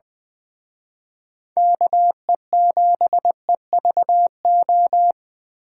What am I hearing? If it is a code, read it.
Morse code, heard as a keyed tone, about 24 wpm, 700 Hz. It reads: KE7EVO
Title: Speed Racing - USA 2x3 Call Signs 20wpm